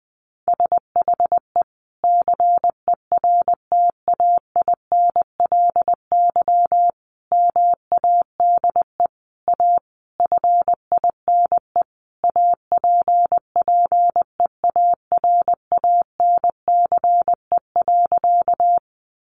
Morse code, heard as a keyed tone, about 20 wpm, 700 Hz. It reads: SHE CERTAINLY MADE A FINE APPEARANCE.
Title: Sentences from Top 1100 Words 20wpm